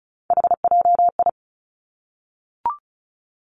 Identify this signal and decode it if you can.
Morse code, heard as a keyed tone, about 36 wpm, 700 Hz. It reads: HJI
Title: Three Challenging Letter-Number Combinations 35wpm